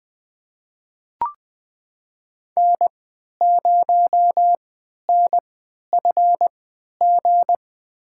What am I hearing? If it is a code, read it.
Morse code, heard as a keyed tone, about 21 wpm, spaced as for 11 wpm, 700 Hz. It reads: N0NFG
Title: USA Call Signs - 5 Characters - 3x Character Spacing - 20wpm